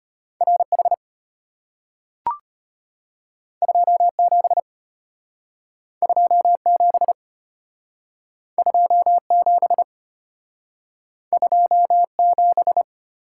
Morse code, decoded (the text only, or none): RH 27 27 27 27